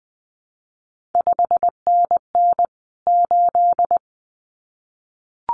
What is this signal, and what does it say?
Morse code, heard as a keyed tone, about 20 wpm, 700 Hz. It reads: 5NN 8
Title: CQ WW DX Contest Exchange - 20wpm